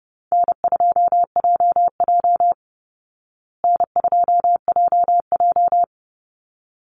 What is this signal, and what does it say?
Morse code, heard as a keyed tone, about 30 wpm, 700 Hz. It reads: N2JJ N2JJ
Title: Easy Call Signs Repeated 2x - 30wpm